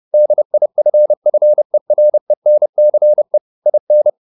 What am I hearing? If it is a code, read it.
Morse code, heard as a keyed tone, about 30 wpm, 600 Hz. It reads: DIFFERENCE IN
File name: the old man and the sea / the old man and the sea 1 30wpm